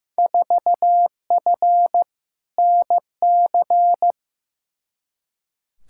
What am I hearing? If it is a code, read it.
Morse code, heard as a keyed tone, about 15 wpm, 700 Hz. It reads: 4F NC
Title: Field Day Exchanges - 15wpm